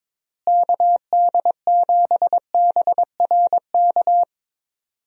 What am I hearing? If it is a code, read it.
Morse code, heard as a keyed tone, about 22 wpm, 700 Hz. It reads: KD7BRK
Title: USA Call Signs - 6 Characters - Repeated 3x - 22wpm